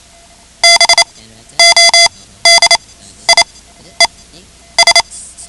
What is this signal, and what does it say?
Morse code, heard as a keyed tone, about 28 wpm, spaced as for 12 wpm, 2050 Hz. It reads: BODIES
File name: ham radio / CodeQuick / Code Quick 10